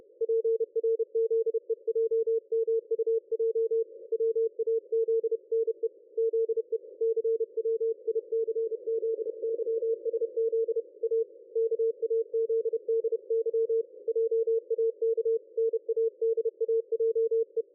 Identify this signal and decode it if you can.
Morse code, heard as a keyed tone, about 31 wpm, 450 Hz. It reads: PRZEJMUJ WAZNE ZE CWICZYSZ A KAZDY JAK NADAJE
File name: dlazaawansowanych / nagraniazpasma / ExpertSDR2 line out 23.01.2020 14.38.27